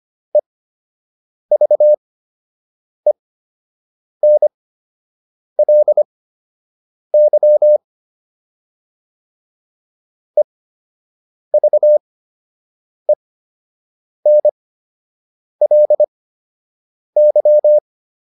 Morse code, heard as a keyed tone, about 25 wpm, spaced as for 7 wpm, 600 Hz. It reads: EVENLY EVENLY